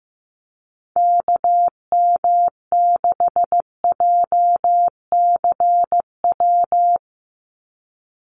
Morse code, heard as a keyed tone, about 15 wpm, 700 Hz. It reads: KM6JCW
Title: USA Call Signs 15wpm